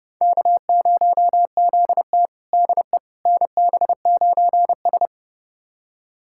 Morse code, heard as a keyed tone, about 30 wpm, 700 Hz. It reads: K0ZT DE NB9S
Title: USA Callsign DE Callsign - 4 Characters - 30wpm